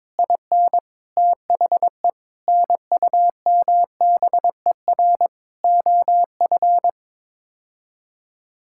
Morse code, heard as a keyed tone, about 22 wpm, 700 Hz. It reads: IN THE NUMBER OF
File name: practice / 22wpm / Build A Phrase - 5-Word Phrases - 22wpm